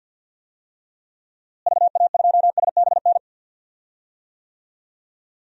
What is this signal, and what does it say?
Morse code, heard as a keyed tone, about 53 wpm, 700 Hz. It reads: VA2SBN
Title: DXCC-100 Call Signs - 6 Characters - Repeated 2x - 50wpm